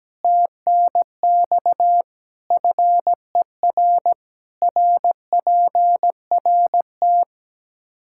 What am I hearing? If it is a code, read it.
Morse code, heard as a keyed tone, about 17 wpm, 700 Hz. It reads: TNX FER RPRT
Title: K1USN SST QSO Practice 17wpm